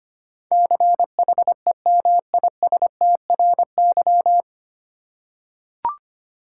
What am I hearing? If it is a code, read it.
Morse code, heard as a keyed tone, about 25 wpm, 700 Hz. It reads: CHEMISTRY
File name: Nine Letter Words - 25wpm